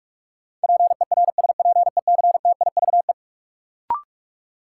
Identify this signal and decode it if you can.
Morse code, heard as a keyed tone, about 46 wpm, 700 Hz. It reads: PERSPECTIVE
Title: Sentences from Top 1200 Words - New Word Drill - 45wpm